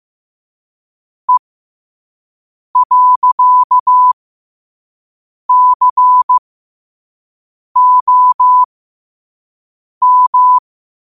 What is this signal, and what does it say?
Morse code, heard as a keyed tone, about 15 wpm, spaced as for 5 wpm, 1000 Hz. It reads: E.COM
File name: qotdfiles / 5wpm